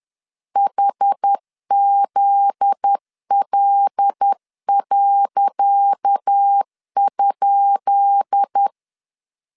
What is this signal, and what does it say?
Morse code, heard as a keyed tone, about 10 wpm, 800 Hz. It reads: HZL.?
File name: LEZIONI / 10 SIGNS / 10C DOT 10wpm 2940